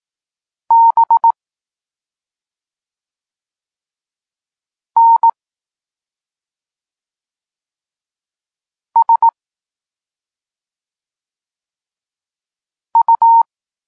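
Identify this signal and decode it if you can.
Morse code, heard as a keyed tone, about 18 wpm, 920 Hz. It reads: B N S U
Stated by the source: Morse code lesson for learning the letters K and X, from the SP9MOA Niepołomicki Klub Krótkofalowców course.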